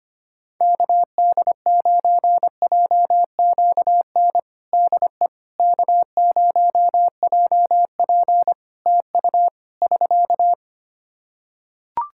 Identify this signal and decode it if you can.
Morse code, heard as a keyed tone, about 25 wpm, 700 Hz. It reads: KD9JQN DE K0JP TU <SK>
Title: Practice QSO - Exchange 4 - Ending - 25wpm